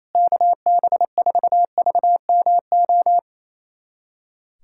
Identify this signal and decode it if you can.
Morse code, heard as a keyed tone, about 28 wpm, 700 Hz. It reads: KB4VMO
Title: USA Callsign DE Callsign - 6 Characters - 28wpm